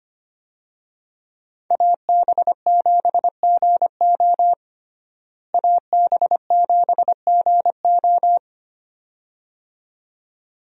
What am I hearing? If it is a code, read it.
Morse code, heard as a keyed tone, about 25 wpm, 700 Hz. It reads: AB7GO AB7GO
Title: USA Call Signs - 5 Characters - Repeated 2x - 25wpm